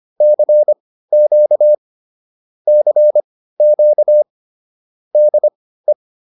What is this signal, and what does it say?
Morse code, heard as a keyed tone, about 25 wpm, spaced as for 15 wpm, 600 Hz. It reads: CQ CQ DE
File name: QSO107 15